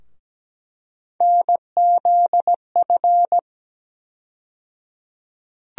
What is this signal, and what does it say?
Morse code, heard as a keyed tone, about 17 wpm, 700 Hz. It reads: NZF
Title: USA Call Sign Postfixes - 3 Letters 17wpm